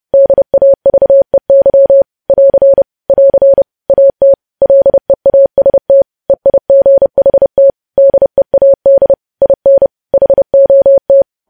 Morse code, heard as a keyed tone, about 30 wpm, 550 Hz. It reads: DAVEY <AR> <AR> AT LEAST EIGHT DEAD IN HOT
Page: news headlines in morse code at 30 wpm